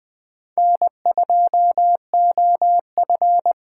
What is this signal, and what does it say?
Morse code, heard as a keyed tone, about 20 wpm, 700 Hz. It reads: N2OF